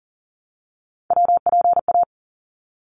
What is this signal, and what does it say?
Morse code, heard as a keyed tone, about 40 wpm, 700 Hz. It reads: WPA
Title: ARRL Sections 40wpm